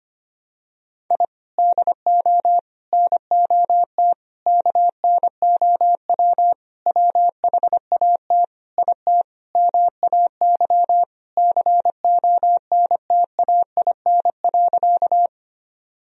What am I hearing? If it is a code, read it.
Morse code, heard as a keyed tone, about 25 wpm, 700 Hz. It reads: I DO NOT KNOW WHAT IT MAY CONTAIN.
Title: Sentences from Top 1000 Words - Review Part 2 - 25wpm